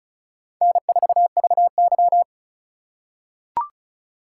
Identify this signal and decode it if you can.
Morse code, heard as a keyed tone, about 34 wpm, 700 Hz. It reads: N4VY
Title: USA Call Signs - 4x Character Spacing - 35wpm